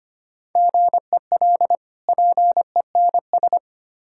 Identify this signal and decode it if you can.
Morse code, heard as a keyed tone, about 25 wpm, 700 Hz. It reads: GEL PENS
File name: practice / 25wpm / Difficult Words with Hints - 25wpm